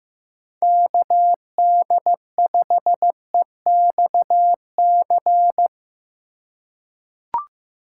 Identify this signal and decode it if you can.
Morse code, heard as a keyed tone, about 15 wpm, 700 Hz. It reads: KD5EXC
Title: USA Call Signs - 6 Characters - 2x Character Spacing - 15wpm